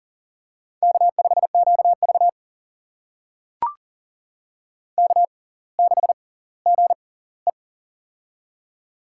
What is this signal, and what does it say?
Morse code, heard as a keyed tone, about 40 wpm, spaced as for 21 wpm, 700 Hz. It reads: K5QV K6GE
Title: USA Call Signs - 4 Characters - 6x Character Spacing - 40wpm